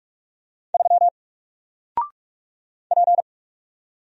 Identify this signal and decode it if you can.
Morse code, heard as a keyed tone, about 44 wpm, 700 Hz. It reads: 3 P
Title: Single Character (Letter:Number:Punctuation:Prosign) 45wpm